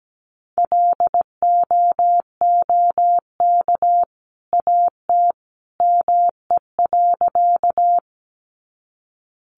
Morse code, heard as a keyed tone, about 17 wpm, 700 Hz. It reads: LOOK AT ME.
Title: Sentences from Top 200 Words 17wpm